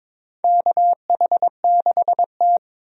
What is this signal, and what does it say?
Morse code, heard as a keyed tone, about 21 wpm, 700 Hz. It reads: KH6T
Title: USA Call Signs - 4 Characters - 22wpm